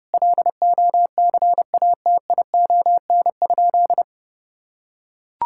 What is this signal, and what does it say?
Morse code, heard as a keyed tone, about 30 wpm, 700 Hz. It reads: LOCATION?
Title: Sentences from Top 1000 Words - Review Part 6 - 30wpm